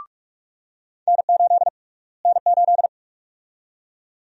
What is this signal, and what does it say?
Morse code, heard as a keyed tone, about 46 wpm, 700 Hz. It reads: N8 N8
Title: USA Call Sign Prefixes - Repeated 2x - 45wpm